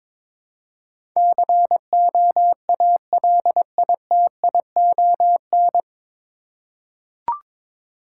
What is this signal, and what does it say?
Morse code, heard as a keyed tone, about 22 wpm, 690 Hz. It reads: COALITION
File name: 9 Letter Words - First 3 Letters Given - 22wpm